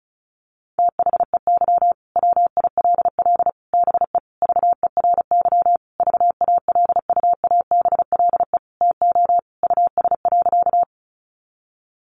Morse code, heard as a keyed tone, about 34 wpm, 700 Hz. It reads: THEY WILL BE VERY VALUABLE TO US.
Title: Sentences from Top 1100 Words 35wpm